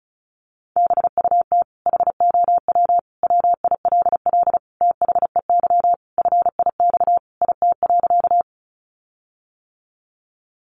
Morse code, heard as a keyed tone, about 34 wpm, 700 Hz. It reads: BUT HOW WILL THEY FIX IT.
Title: Sentences from Top 1400 Words 35wpm